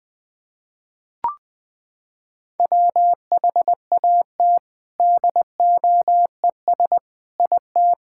Morse code, heard as a keyed tone, about 20 wpm, 700 Hz. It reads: WHAT DOES IT
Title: Sets of 6 Words - 20wpm